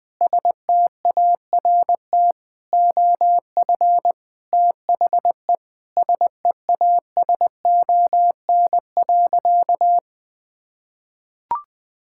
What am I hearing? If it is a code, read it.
Morse code, heard as a keyed tone, about 20 wpm, 700 Hz. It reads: START OF THE SEASON.
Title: Sentences with 6-Word Phrases - 20wpm